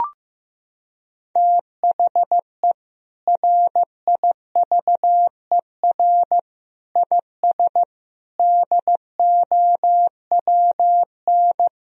Morse code, heard as a keyed tone, about 15 wpm, 700 Hz. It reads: THE RIVER IS DOWN
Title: Sentences from Top 1000 Words - Review Part 1 - 15wpm